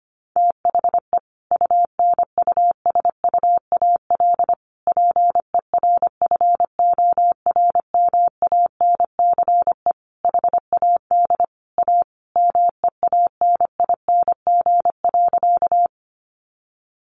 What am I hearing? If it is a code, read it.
Morse code, heard as a keyed tone, about 25 wpm, 700 Hz. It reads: THE UNUSUAL PERFORMANCE HAD A MEANING.
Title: Sentences from Top 1400 Words 25wpm